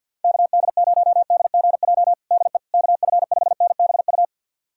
Morse code, heard as a keyed tone, about 52 wpm, 700 Hz. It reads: KD0DGJ DE KF5NBU
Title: USA Callsign DE Callsign - 6 Characters - 50wpm